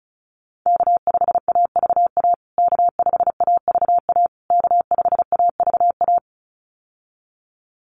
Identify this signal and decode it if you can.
Morse code, heard as a keyed tone, about 34 wpm, 700 Hz. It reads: K5AVA K5AVA K5AVA
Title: USA Call Signs Repeated 3x 35wpm